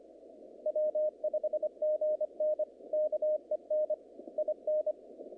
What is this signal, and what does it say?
Morse code, heard as a keyed tone, about 24 wpm, 610 Hz, with radio noise. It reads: W5GN KEN IN
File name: contests / naqp-cw-k5tr-op-k5pi- -2300